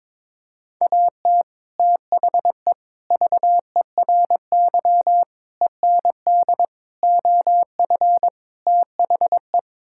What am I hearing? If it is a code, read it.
Morse code, heard as a keyed tone, about 22 wpm, 700 Hz. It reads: AT THE VERY END OF THE
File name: practice / 22wpm / Sets of 6 Words - 22wpm